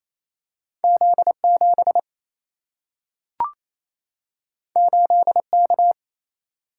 Morse code, heard as a keyed tone, about 28 wpm, 700 Hz. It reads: Z 7 8 K
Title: Single-Letter Single-Number 28wpm